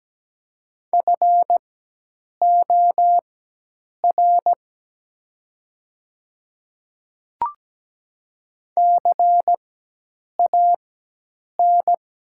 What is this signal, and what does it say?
Morse code, heard as a keyed tone, about 17 wpm, spaced as for 8 wpm, 700 Hz. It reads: FOR CAN
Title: 4x Character Spacing - Top 200 Words - 17wpm